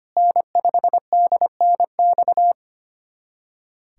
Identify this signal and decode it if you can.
Morse code, heard as a keyed tone, about 25 wpm, 700 Hz. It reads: N5DNX
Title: USA Callsign DE Callsign - 5 Characters - 25wpm